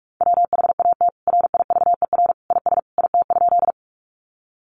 Morse code, heard as a keyed tone, about 44 wpm, 700 Hz. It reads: WHAT RIVER IS IT?
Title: Sentences from Top 300 Words 45wpm